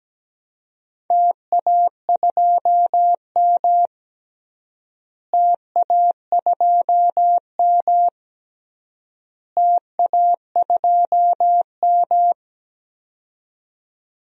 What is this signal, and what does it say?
Morse code, heard as a keyed tone, about 17 wpm, 700 Hz. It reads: TA2M TA2M TA2M
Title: DXCC-100 Call Signs - 4 Characters - Repeated 3x - 17wpm